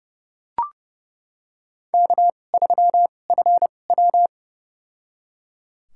Morse code, heard as a keyed tone, about 30 wpm, spaced as for 22 wpm, 700 Hz. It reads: K3FW
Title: USA Call Signs - 4 Characters - 2x Character Spacing - 30wpm